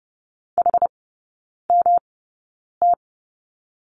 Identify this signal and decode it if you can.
Morse code, heard as a keyed tone, about 30 wpm, spaced as for 9 wpm, 700 Hz. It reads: HMT
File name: Lesson 18e - Letter m - Three Characters 30wpm